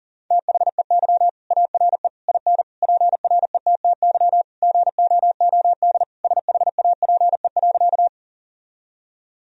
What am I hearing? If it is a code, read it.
Morse code, heard as a keyed tone, about 40 wpm, 700 Hz. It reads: THEY ARE IN PRETTY GOOD SHAPE.
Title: Sentences from Top 1000 Words - Review Part 3 - 40wpm